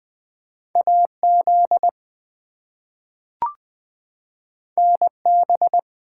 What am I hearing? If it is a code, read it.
Morse code, heard as a keyed tone, about 20 wpm, 700 Hz. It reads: AZ NB
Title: US States and Canadian Provinces 20wpm